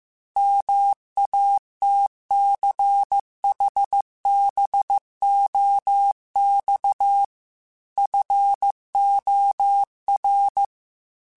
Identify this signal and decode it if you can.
Morse code, heard as a keyed tone, about 15 wpm, 800 Hz. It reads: MATCHBOX FOR
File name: QSOs / QSOs-mp3s / WD0FBT 1@13